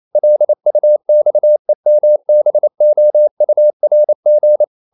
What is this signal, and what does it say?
Morse code, heard as a keyed tone, about 28 wpm, 600 Hz. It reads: LUXEMBOURG
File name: CW / Practice / NationalCapitals28